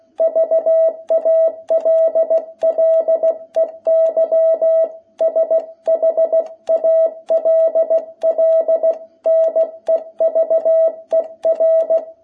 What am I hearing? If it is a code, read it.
Morse code, heard as a keyed tone, about 16 wpm, 660 Hz. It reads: VALLEY SHALL NEVER
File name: jh8cbh / 2021cw